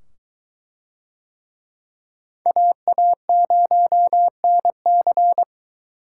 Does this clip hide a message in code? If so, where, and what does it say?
Yes: Morse code from 2.5 to 5.4 s, heard as a keyed tone, about 22 wpm, 700 Hz. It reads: AA0NC